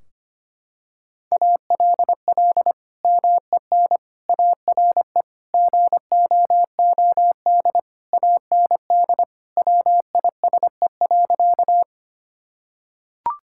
Morse code, heard as a keyed tone, about 25 wpm, 700 Hz. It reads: ALL MEN ARE GOOD AND WISE.
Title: Sentences from Top 1600 Words 25wpm